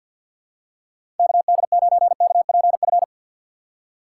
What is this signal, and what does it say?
Morse code, heard as a keyed tone, about 50 wpm, 700 Hz. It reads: KD9KPF
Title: USA Call Signs - 2x Character Spacing - 50wpm